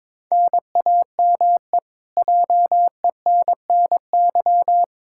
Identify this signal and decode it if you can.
Morse code, heard as a keyed tone, about 22 wpm, 700 Hz. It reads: NAME JENNY
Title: Practice QSO - Exchange 1 - Beginning - 22wpm